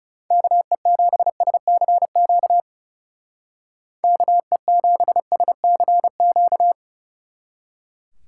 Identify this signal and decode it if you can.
Morse code, heard as a keyed tone, about 38 wpm, 700 Hz. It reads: KE7SCQ KE7SCQ
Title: Speed Racing - USA 2x3 Call Signs 30wpm